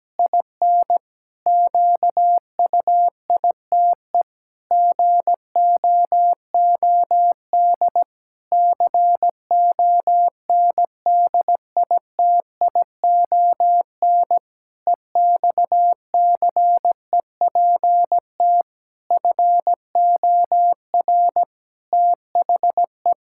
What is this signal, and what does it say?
Morse code, heard as a keyed tone, about 17 wpm, 700 Hz. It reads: IN QUITE GOOD CONDITION EXCEPT FOR THE
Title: Pangram Challenge v2 - 17wpm